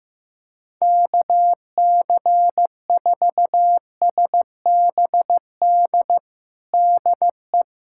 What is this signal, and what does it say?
Morse code, heard as a keyed tone, about 15 wpm, 700 Hz. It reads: KC4SBD DE
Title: USA Callsign DE Callsign - 6 Characters - 15wpm